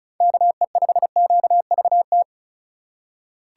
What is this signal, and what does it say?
Morse code, heard as a keyed tone, about 34 wpm, 700 Hz. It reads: KE5QVT
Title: USA Call Signs - 6 Characters - 4x Character Spacing - 35wpm